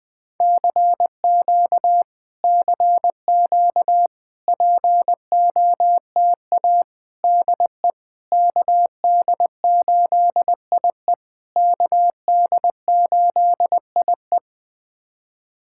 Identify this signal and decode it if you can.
Morse code, heard as a keyed tone, about 20 wpm, 700 Hz. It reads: CQ CQ POTA DE KD8IE KD8IE
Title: POTA QSO Practice - 20wpm